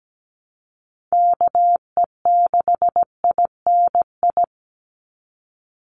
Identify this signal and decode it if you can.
Morse code, heard as a keyed tone, about 17 wpm, 700 Hz. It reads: KE6INI
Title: USA Call Signs 17wpm